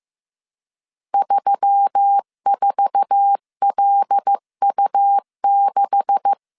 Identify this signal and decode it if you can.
Morse code, heard as a keyed tone, about 15 wpm, 800 Hz. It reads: 34LU6
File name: CW / DET / 1 9 1500C S B 14wpm 3302